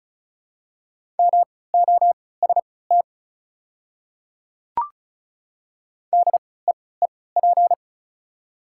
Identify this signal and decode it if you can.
Morse code, heard as a keyed tone, about 36 wpm, spaced as for 20 wpm, 690 Hz. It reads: MOST DEEP
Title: 3x Character Spacing - Top 500 Words - 35wpm